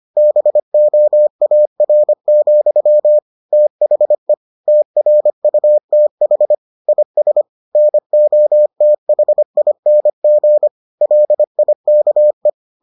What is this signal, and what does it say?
Morse code, heard as a keyed tone, about 25 wpm, 600 Hz. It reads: BOAR, THE TRUTH IS NOTHING LIKE